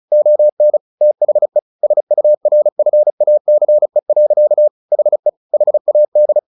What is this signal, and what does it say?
Morse code, heard as a keyed tone, about 34 wpm, 600 Hz. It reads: ON THE SURFACE. HE HAD
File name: the old man and the sea / the old man and the sea 1 35wpm